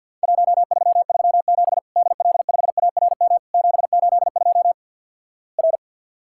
Morse code, heard as a keyed tone, about 50 wpm, 700 Hz. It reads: 1337 DL5ARM 782 R
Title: Worked All Europe DX Contest - QTC Exchange with Repeats - 50wpm